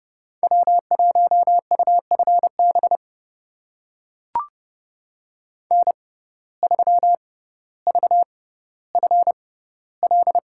Morse code, heard as a keyed tone, about 30 wpm, spaced as for 15 wpm, 700 Hz. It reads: W1UFB N3VFL
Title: USA Call Signs - 5 Characters - 6x Character Spacing - 30wpm